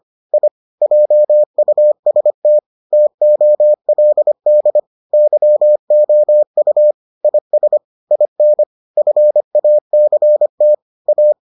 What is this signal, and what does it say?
Morse code, heard as a keyed tone, about 25 wpm, 600 Hz. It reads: I JUST TOLD YOU IS IN FACT A